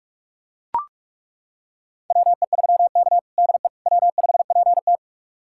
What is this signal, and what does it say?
Morse code, heard as a keyed tone, about 43 wpm, 700 Hz. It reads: WE3K DE W5PT
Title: USA Callsign DE Callsign - 4 Characters - 45wpm